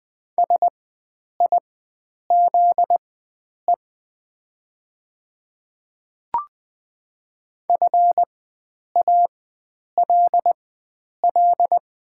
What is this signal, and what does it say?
Morse code, heard as a keyed tone, about 20 wpm, spaced as for 9 wpm, 700 Hz. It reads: SIZE FALL
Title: 4x Character Spacing - Top 500 Words - 20wpm